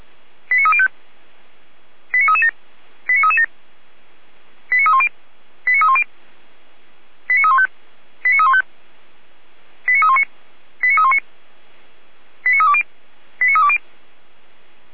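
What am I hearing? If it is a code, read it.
Five-tone selective call (ZVEI), heard as quick runs of five tones, tones of about 70 ms. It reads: 89305 89337 89337 89310 89310 89315 89315 89319 89319 89320 89320
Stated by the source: ZVEI five-tone selective call received by SDR.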